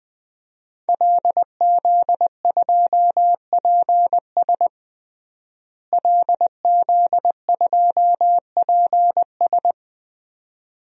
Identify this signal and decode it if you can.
Morse code, heard as a keyed tone, about 20 wpm, 700 Hz. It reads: LZ2PS LZ2PS
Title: DXCC-100 Call Signs - 5 Characters - Repeated 2x - 20wpm